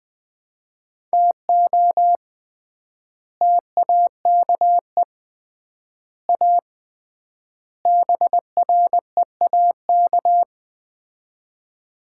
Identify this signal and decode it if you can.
Morse code, heard as a keyed tone, about 20 wpm, 700 Hz. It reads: TO TAKE A BREAK
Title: 3x Word-Spacing - Sets of 4 Words - 20wpm